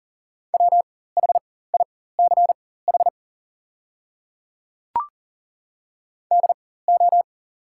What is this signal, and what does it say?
Morse code, heard as a keyed tone, about 40 wpm, spaced as for 19 wpm, 690 Hz. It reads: WHICH DO
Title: 4x Character Spacing - Top 100 Words - 40wpm